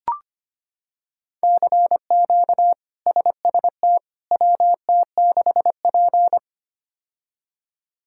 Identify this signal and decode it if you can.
Morse code, heard as a keyed tone, about 25 wpm, 700 Hz. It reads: CQ SST WT6P
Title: K1USN SST QSO Practice - 25wpm